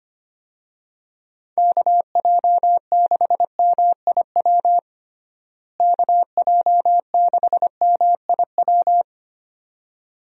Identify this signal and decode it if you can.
Morse code, heard as a keyed tone, about 25 wpm, 700 Hz. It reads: KJ6MIW KJ6MIW
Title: USA Call Signs - 6 Characters - Repeated 2x - 25wpm